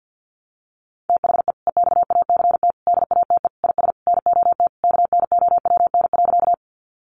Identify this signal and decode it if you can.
Morse code, heard as a keyed tone, about 50 wpm, 700 Hz. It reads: THE EXACT DATE IS NOT KNOWN.
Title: Sentences from Top 1300 Words 50wpm